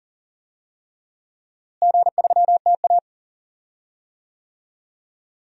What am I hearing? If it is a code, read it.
Morse code, heard as a keyed tone, about 40 wpm, 700 Hz. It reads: G3TA
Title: DXCC-100 Call Signs - 4 Characters - Repeated 3x - 40wpm